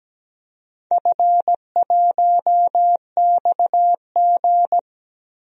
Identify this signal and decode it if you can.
Morse code, heard as a keyed tone, about 17 wpm, 700 Hz. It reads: F1XG
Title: DXCC-100 Call Signs - 4 Characters - 17wpm